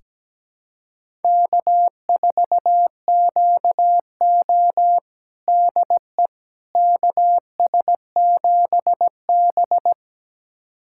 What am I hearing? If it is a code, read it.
Morse code, heard as a keyed tone, about 17 wpm, 700 Hz. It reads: K4QO DE KS7B
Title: USA Callsign DE Callsign - 4 Characters - 17wpm